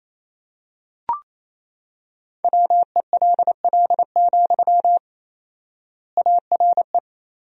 Morse code, heard as a keyed tone, about 28 wpm, 700 Hz. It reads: WELL, ARE
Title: 4x Word-Spacing - Sentences from Top 400 Words - 28wpm